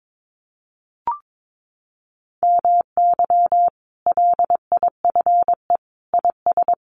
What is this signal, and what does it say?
Morse code, heard as a keyed tone, about 22 wpm, 700 Hz. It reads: MY LIFE IS
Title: Sentences from Top 1400 Words 22wpm